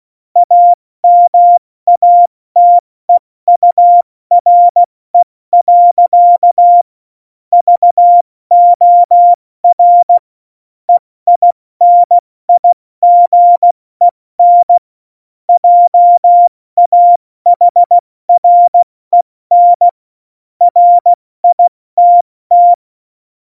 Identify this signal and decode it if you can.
Morse code, heard as a keyed tone, about 16 wpm, 700 Hz. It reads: AMATEURE. VOR EINIGEN JAHREN RITT
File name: Morsekurs / Witze / Witz 01. Dezember 2020, 70 F80